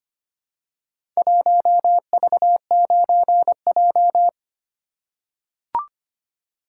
Morse code, heard as a keyed tone, about 25 wpm, 700 Hz. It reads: 1V9J
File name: practice / 25wpm / Four Letter-Number Combinations - 25wpm